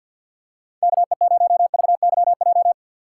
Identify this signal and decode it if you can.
Morse code, heard as a keyed tone, about 49 wpm, 700 Hz. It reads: KE0VYJ
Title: USA Call Signs - 6 Characters - 4x Character Spacing - 50wpm